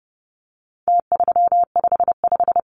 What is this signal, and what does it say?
Morse code, heard as a keyed tone, about 30 wpm, 700 Hz. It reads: T355
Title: Numbers - Contest Style - 4-Digit 30wpm